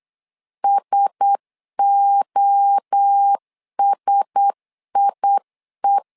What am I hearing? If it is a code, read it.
Morse code, heard as a keyed tone, about 8 wpm, 800 Hz. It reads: SOSIE
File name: LEZIONI / 4 Q1S 8wpm 2710